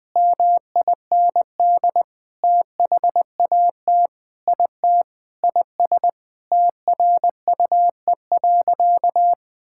Morse code, heard as a keyed tone, about 20 wpm, 700 Hz. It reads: MIND THAT IT IS TRUE.
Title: Sentences with 6-Word Phrases - 20wpm